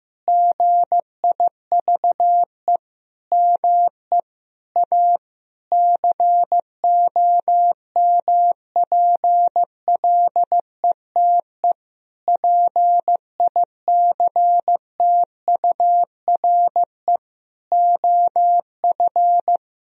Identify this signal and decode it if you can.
Morse code, heard as a keyed tone, about 15 wpm, 700 Hz. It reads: GIVE ME A COMPLETE PICTURE OF
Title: Sentences from Top 1000 Words - Review Part 2 - 15wpm